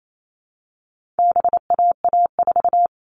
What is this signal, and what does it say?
Morse code, heard as a keyed tone, about 28 wpm, 700 Hz. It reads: BAA4
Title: All Cut Numbers - 4-Digit 28wpm